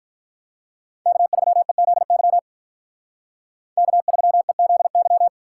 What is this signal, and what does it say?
Morse code, heard as a keyed tone, about 44 wpm, 700 Hz. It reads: K3EZY K3EZY